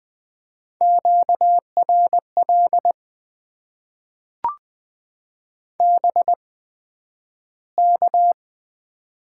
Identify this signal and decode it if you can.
Morse code, heard as a keyed tone, about 20 wpm, spaced as for 11 wpm, 700 Hz. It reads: QRL BK
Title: Top 90 QSO Elements - 8x Character Spacing - 20wpm